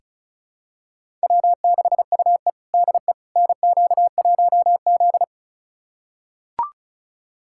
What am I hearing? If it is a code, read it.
Morse code, heard as a keyed tone, about 34 wpm, 700 Hz. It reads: W6UE DE NQ1Z
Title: USA Callsign DE Callsign - 4 Characters - 35wpm